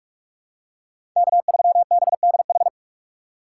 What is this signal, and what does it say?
Morse code, heard as a keyed tone, about 46 wpm, 700 Hz. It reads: K3BDH
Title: USA Call Signs - 5 Characters - 3x Character Spacing - 45wpm